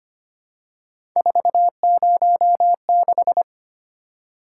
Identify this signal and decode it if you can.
Morse code, heard as a keyed tone, about 25 wpm, 700 Hz. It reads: 406
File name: Three Letter-Number Combinations - 25wpm